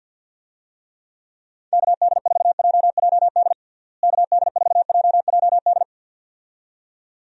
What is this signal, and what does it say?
Morse code, heard as a keyed tone, about 50 wpm, 700 Hz. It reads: KD4JJD KD4JJD
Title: USA Call Signs - 6 Characters - Repeated 2x - 50wpm